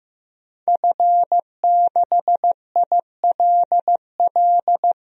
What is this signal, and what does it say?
Morse code, heard as a keyed tone, about 15 wpm, 700 Hz. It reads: F6ILL
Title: DXCC-100 Call Signs - 5 Characters - Repeated 2x - 15wpm